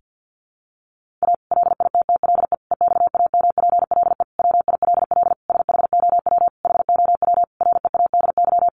Morse code, heard as a keyed tone, about 48 wpm, 700 Hz. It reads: A LITTLE EXAMPLE WILL SHOW HOW READY
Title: Sentences from Top 1500 Words 50wpm